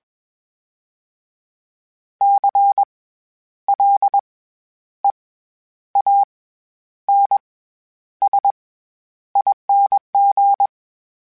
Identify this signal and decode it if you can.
Morse code, heard as a keyed tone, about 20 wpm, 800 Hz. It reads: C L E A N S ING
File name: Special INGs-mp3s / Special INGs-mp3s / SlowToFaster ing @20